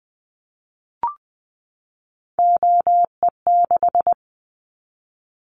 Morse code, heard as a keyed tone, about 19 wpm, 700 Hz. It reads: OE6
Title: Callsign Prefixes for Top 50 DXCC - 20wpm